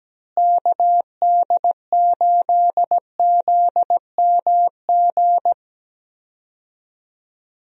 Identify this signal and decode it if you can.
Morse code, heard as a keyed tone, about 17 wpm, 700 Hz. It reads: KD8ZMG
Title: USA Call Signs - 6 Characters - Repeated 2x - 17wpm